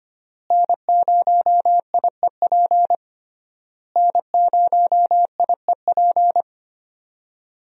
Morse code, heard as a keyed tone, about 25 wpm, 700 Hz. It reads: N0IEP N0IEP
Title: USA Call Signs - 5 Characters - Repeated 3x - 25wpm